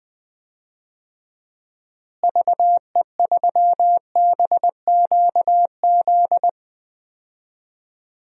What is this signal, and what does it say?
Morse code, heard as a keyed tone, about 20 wpm, 700 Hz. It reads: VE3BQZ
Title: DXCC-100 Call Signs - 6 Characters - Repeated 3x - 20wpm